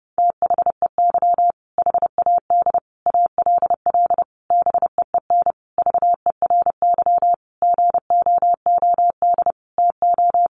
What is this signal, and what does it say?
Morse code, heard as a keyed tone, about 30 wpm, 700 Hz. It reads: THEY HAD ALL BEEN VERY GOOD TO
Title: Sentences from Top 200 Words 30wpm